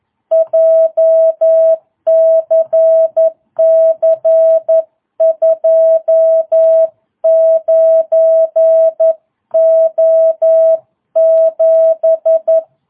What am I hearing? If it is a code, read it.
Morse code, heard as a keyed tone, about 11 wpm, 660 Hz. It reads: JCC 29O7